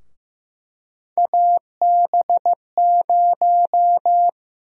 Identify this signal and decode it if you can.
Morse code, heard as a keyed tone, about 15 wpm, 700 Hz. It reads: AB0
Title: USA Call Sign Prefixes - 15wpm